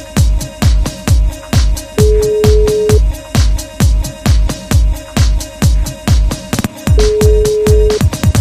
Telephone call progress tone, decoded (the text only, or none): ringback tone